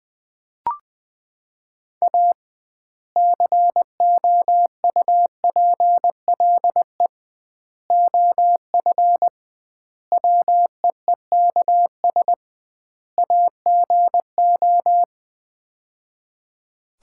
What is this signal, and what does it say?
Morse code, heard as a keyed tone, about 20 wpm, 700 Hz. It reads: A COUPLE OF WEEKS AGO
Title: 2x Word-Spacing - Sets of 5 Words - 20wpm